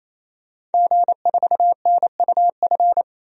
Morse code, heard as a keyed tone, about 28 wpm, 700 Hz. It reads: G4NUF
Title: DXCC-100 Call Signs - 5 Characters - 28wpm